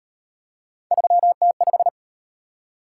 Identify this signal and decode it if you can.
Morse code, heard as a keyed tone, about 36 wpm, 700 Hz. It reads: 3T5